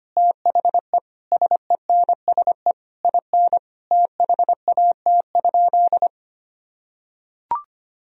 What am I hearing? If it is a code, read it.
Morse code, heard as a keyed tone, about 25 wpm, 700 Hz. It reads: THE SENSE IN THAT?
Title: Sentences from Top 1000 Words - Review Part 4 - 25wpm